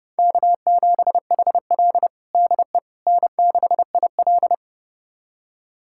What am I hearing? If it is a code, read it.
Morse code, heard as a keyed tone, about 32 wpm, 700 Hz. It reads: K7HL DE N6IL
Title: USA Callsign DE Callsign - 4 Characters - 30wpm